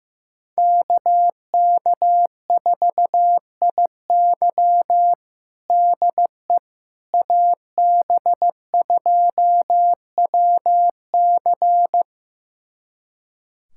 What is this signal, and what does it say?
Morse code, heard as a keyed tone, about 15 wpm, 700 Hz. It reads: KK4IY DE AB2WC
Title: USA Callsign DE Callsign - 5 Characters - 15wpm